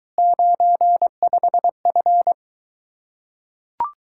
Morse code, heard as a keyed tone, about 24 wpm, 700 Hz. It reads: 95F